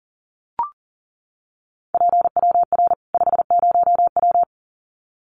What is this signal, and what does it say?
Morse code, heard as a keyed tone, about 40 wpm, 700 Hz. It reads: PWR 50W
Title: Common QSO Phrases 40wpm